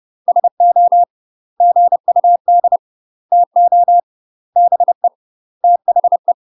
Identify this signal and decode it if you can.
Morse code, heard as a keyed tone, about 30 wpm, 700 Hz. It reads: SO GUD TO BE THE
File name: UK MASTER QSOs TRAINING / 30WPM / UK MASTER QSOs 0019